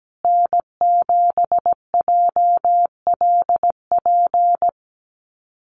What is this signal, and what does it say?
Morse code, heard as a keyed tone, about 17 wpm, 700 Hz. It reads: N7JLP
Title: USA Call Signs Repeated 2x 17wpm